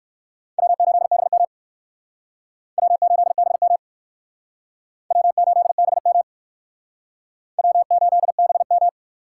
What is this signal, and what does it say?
Morse code, heard as a keyed tone, about 57 wpm, 700 Hz. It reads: W8BM W8BM W8BM W8BM